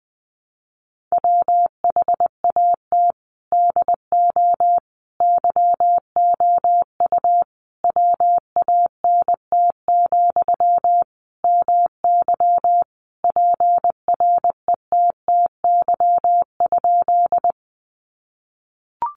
Morse code, heard as a keyed tone, about 20 wpm, 700 Hz. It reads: WHAT DO YOU WANT, MY PRETTY?
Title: Sentences from Top 1000 Words 20wpm